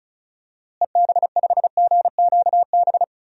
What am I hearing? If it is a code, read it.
Morse code, heard as a keyed tone, about 36 wpm, 700 Hz. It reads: EB5GQB
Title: DXCC-100 Call Signs - 6 Characters - Repeated 3x - 35wpm